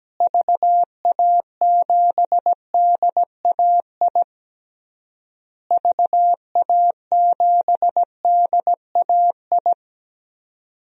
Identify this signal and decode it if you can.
Morse code, heard as a keyed tone, about 17 wpm, 700 Hz. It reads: VA7DAI VA7DAI
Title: DXCC-100 Call Signs - 6 Characters - Repeated 3x - 17wpm